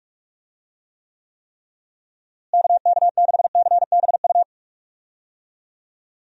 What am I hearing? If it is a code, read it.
Morse code, heard as a keyed tone, about 46 wpm, 700 Hz. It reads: KK6CBU
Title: USA Call Signs - 6 Characters - Repeated 3x - 45wpm